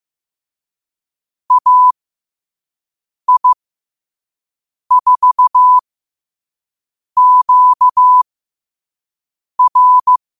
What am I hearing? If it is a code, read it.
Morse code, heard as a keyed tone, about 15 wpm, spaced as for 5 wpm, 1000 Hz. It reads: AI4QR
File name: qotdfiles / 5wpm